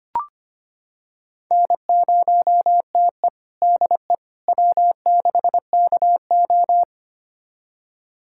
Morse code, heard as a keyed tone, about 25 wpm, 700 Hz. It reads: N0TE DE W6KO
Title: USA Callsign DE Callsign - 4 Characters - 25wpm